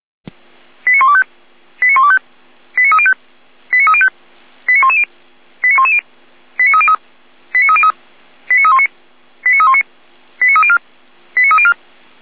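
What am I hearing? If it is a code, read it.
Five-tone selective call (ZVEI), heard as quick runs of five tones, tones of about 70 ms. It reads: 89125 89125 89305 89305 89110 89110 89303 89303 89319 89319 89304 89304
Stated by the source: ZVEI five-tone selective call 89276 received by SDR.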